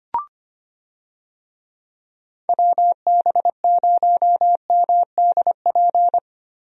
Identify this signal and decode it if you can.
Morse code, heard as a keyed tone, about 25 wpm, 700 Hz. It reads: WB0MDP
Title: USA Call Signs - 6 Characters - Repeated 3x - 25wpm